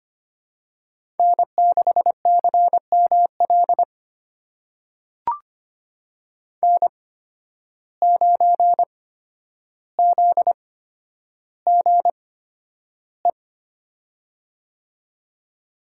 Morse code, heard as a keyed tone, about 25 wpm, spaced as for 11 wpm, 700 Hz. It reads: N6CML N9ZGE
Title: USA Call Signs - 5 Characters - 8x Character Spacing - 25wpm